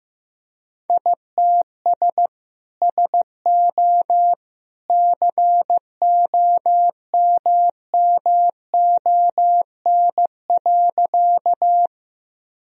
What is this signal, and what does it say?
Morse code, heard as a keyed tone, about 15 wpm, 700 Hz. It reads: ITS SO COMMON.
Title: Sentences from Top 1000 Words - Review Part 3 - 15wpm